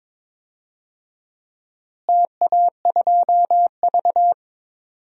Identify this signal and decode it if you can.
Morse code, heard as a keyed tone, about 22 wpm, 700 Hz. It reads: TA2V
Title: DXCC-100 Call Signs - 4 Characters - Repeated 3x - 22wpm